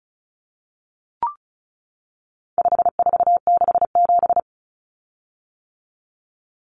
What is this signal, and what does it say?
Morse code, heard as a keyed tone, about 34 wpm, 700 Hz. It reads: 5467
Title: Common Cut Numbers - 4-Digit 35wpm